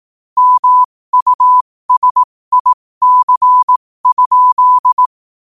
Morse code, heard as a keyed tone, about 18 wpm, spaced as for 15 wpm, 1000 Hz. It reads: MUSIC?
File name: qotdfiles / 15wpm